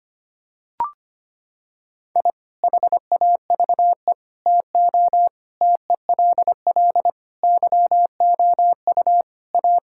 Morse code, heard as a keyed tone, about 25 wpm, 700 Hz. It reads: I HAVE TO TELL YOU A
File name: practice / 25wpm / Sentences with 3-Word Phrases - 25wpm